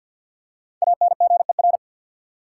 Morse code, heard as a keyed tone, about 53 wpm, 700 Hz. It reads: ANGER
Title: Five Letter Words - 50wpm